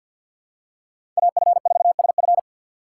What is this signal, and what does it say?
Morse code, heard as a keyed tone, about 49 wpm, 690 Hz. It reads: AF4SF
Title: USA Call Signs - 5 Characters - 6x Character Spacing - 50wpm